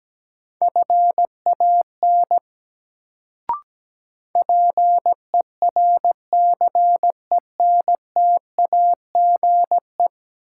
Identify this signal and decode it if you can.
Morse code, heard as a keyed tone, about 17 wpm, 700 Hz. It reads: FAN PERCENTAGE
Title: Sentences from Top 1300 Words - New Word Drill Alt - 17wpm